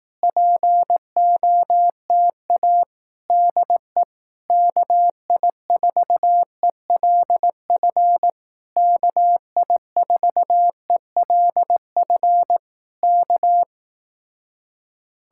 Morse code, heard as a keyed tone, about 18 wpm, 700 Hz. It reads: POTA DE KI4ELF KI4ELF K
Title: Speed-Racing Common QSO Phrases Encore - 15wpm